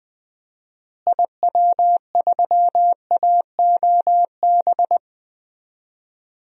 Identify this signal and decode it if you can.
Morse code, heard as a keyed tone, about 20 wpm, 700 Hz. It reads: IW3AOB
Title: DXCC-100 Call Signs - 6 Characters - Repeated 2x - 20wpm